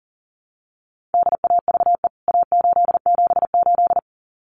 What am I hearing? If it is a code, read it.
Morse code, heard as a keyed tone, about 40 wpm, 700 Hz. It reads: DAVE A878
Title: CWT Exchange 40wpm